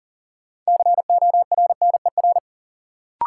Morse code, heard as a keyed tone, about 40 wpm, 700 Hz. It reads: CORNER
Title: Sets of 5 Words - New Word Drill - 40wpm